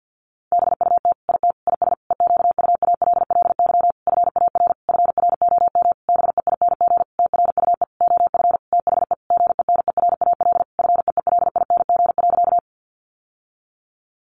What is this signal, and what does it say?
Morse code, heard as a keyed tone, about 50 wpm, 700 Hz. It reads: BUT IT IS EQUALLY FAR FROM BEING TRUE OF THE GENERAL FEELING.
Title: Sentences from Top 1400 Words 50wpm